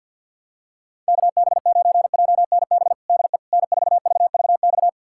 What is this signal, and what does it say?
Morse code, heard as a keyed tone, about 51 wpm, 700 Hz. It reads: KB9JNB DE N4VVX
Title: USA Callsign DE Callsign 50wpm